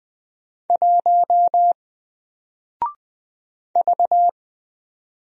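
Morse code, heard as a keyed tone, about 20 wpm, 700 Hz. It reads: 1 V
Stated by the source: Single letter and number Morse drill at 20 wpm, from N4LCD.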